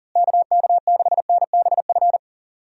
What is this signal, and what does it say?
Morse code, heard as a keyed tone, about 40 wpm, 700 Hz. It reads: KK6NBF
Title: USA Call Signs - 6 Characters - Repeated 2x - 40wpm